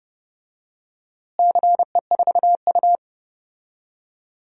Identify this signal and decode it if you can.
Morse code, heard as a keyed tone, about 30 wpm, 700 Hz. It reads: CE4U
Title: DXCC-100 Call Signs - 4 Characters - Repeated 2x - 30wpm